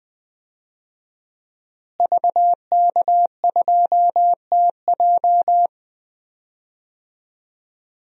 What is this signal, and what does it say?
Morse code, heard as a keyed tone, about 20 wpm, 700 Hz. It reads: VK2TJ
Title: DXCC-100 Call Signs - 5 Characters - Repeated 2x - 20wpm